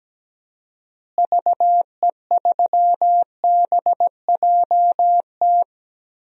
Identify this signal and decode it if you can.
Morse code, heard as a keyed tone, about 17 wpm, 700 Hz. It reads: VE3BJT
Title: DXCC-100 Call Signs - 6 Characters - Repeated 3x - 17wpm